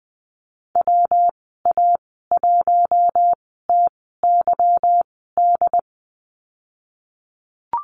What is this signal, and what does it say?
Morse code, heard as a keyed tone, about 20 wpm, spaced as for 15 wpm, 700 Hz. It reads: WA1TYD
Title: 147 Lesson 24g - Letter y - Call Signs 20wpm